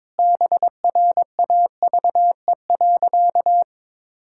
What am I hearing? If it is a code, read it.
Morse code, heard as a keyed tone, about 22 wpm, 700 Hz. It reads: BRAVE.
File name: practice / 22wpm / Sentences with 8-Word Phrases - 22wpm